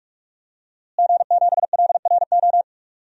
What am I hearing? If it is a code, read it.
Morse code, heard as a keyed tone, about 44 wpm, 700 Hz. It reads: G7LRO
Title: DXCC-100 Call Signs - 5 Characters - Repeated 3x - 45wpm